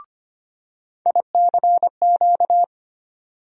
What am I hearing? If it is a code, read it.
Morse code, heard as a keyed tone, about 25 wpm, 700 Hz. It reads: ICQ
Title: USA Call Sign Postfixes - 3 Letters 25wpm